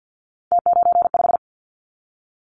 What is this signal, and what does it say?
Morse code, heard as a keyed tone, about 50 wpm, 700 Hz. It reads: T95
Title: Callsign Prefixes for Top 50 DXCC - 50wpm